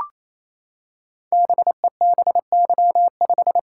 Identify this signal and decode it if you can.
Morse code, heard as a keyed tone, about 27 wpm, 700 Hz. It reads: BEBY5
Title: Code Groups - 5 Letter:Numbers - 28wpm